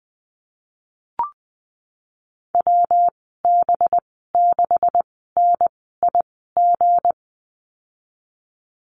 Morse code, heard as a keyed tone, about 20 wpm, spaced as for 15 wpm, 700 Hz. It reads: WB6NIG
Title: 186 Lesson 30g - Letter b - Call Signs 20wpm